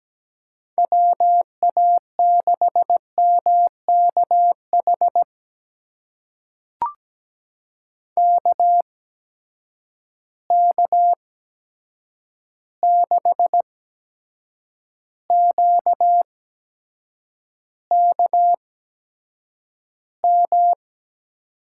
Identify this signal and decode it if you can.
Morse code, heard as a keyed tone, about 17 wpm, spaced as for 7 wpm, 700 Hz. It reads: WA6MKH KK6QKM
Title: USA Call Signs - 6 Characters - 8x Character Spacing - 17wpm